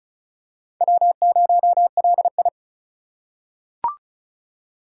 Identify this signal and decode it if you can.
Morse code, heard as a keyed tone, about 36 wpm, 700 Hz. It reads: W0LI
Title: USA Call Signs - 4 Characters - 2x Character Spacing - 35wpm